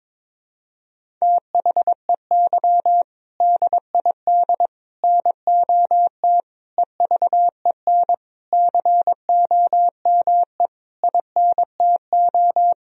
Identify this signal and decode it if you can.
Morse code, heard as a keyed tone, about 22 wpm, 700 Hz. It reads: THEY DID NOT EVEN COME INTO
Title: Sentences from Top 1000 Words - Review Part 1 - 22wpm